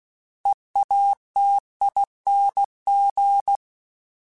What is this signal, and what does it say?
Morse code, heard as a keyed tone, about 16 wpm, 800 Hz. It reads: EATING
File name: Special INGs / Special INGs-mp3s / Sngl ing @15